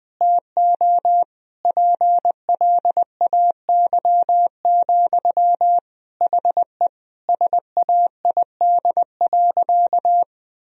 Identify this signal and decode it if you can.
Morse code, heard as a keyed tone, about 20 wpm, 700 Hz. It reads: TO PLAY, HE SAID.
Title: Sentences from Top 1000 Words - Review Part 4 - 20wpm